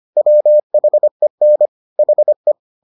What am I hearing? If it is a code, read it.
Morse code, heard as a keyed tone, about 25 wpm, 600 Hz. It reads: WHEN HE